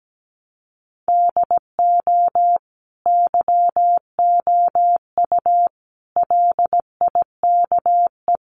Morse code, heard as a keyed tone, about 17 wpm, 700 Hz. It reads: DO YOU LIKE
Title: Sentences from Top 100 Words 17wpm